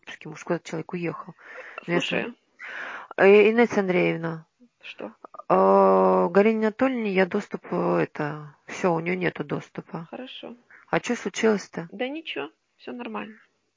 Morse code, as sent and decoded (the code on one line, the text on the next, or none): none